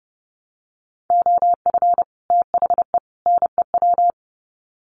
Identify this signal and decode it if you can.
Morse code, heard as a keyed tone, about 30 wpm, 700 Hz. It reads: OF THE NEW
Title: Sets of 3 Words 30wpm